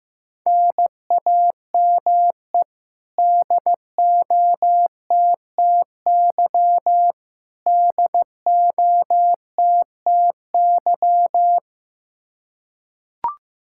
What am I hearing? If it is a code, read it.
Morse code, heard as a keyed tone, about 15 wpm, 700 Hz. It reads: NAME DOTTY DOTTY
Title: Practice QSO - Exchange 1 - Beginning - 15wpm